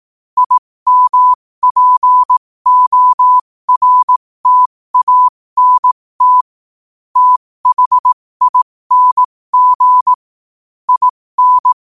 Morse code, heard as a keyed tone, about 18 wpm, spaced as for 15 wpm, 1000 Hz. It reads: IMPORTANT THING IN
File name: qotdfiles / 15wpm